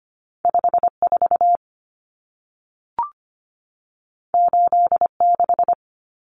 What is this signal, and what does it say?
Morse code, heard as a keyed tone, about 25 wpm, 700 Hz. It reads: 54 86
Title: Numbers - 2-Digit 25wpm